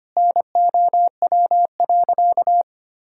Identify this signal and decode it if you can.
Morse code, heard as a keyed tone, about 25 wpm, 700 Hz. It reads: NOW.
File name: practice / 25wpm / Sentences with 2-Word Phrases - 25wpm